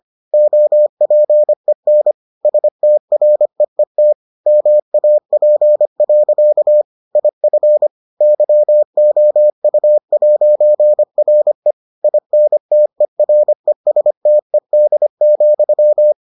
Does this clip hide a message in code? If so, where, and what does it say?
Yes: Morse code from 0.3 to 16.1 s, heard as a keyed tone, about 25 wpm, 600 Hz. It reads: OPEN STREET MAP. IF YOU'RE INTERESTED,